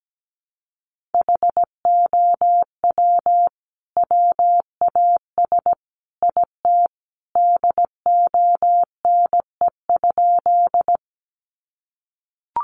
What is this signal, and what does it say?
Morse code, heard as a keyed tone, about 17 wpm, 700 Hz. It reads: HOW WAS IT DONE?
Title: Sentences from Top 500 Words 17wpm